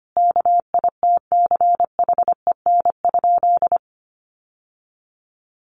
Morse code, heard as a keyed tone, about 25 wpm, 700 Hz. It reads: KITCHEN?
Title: Sentences from Top 1400 Words 25wpm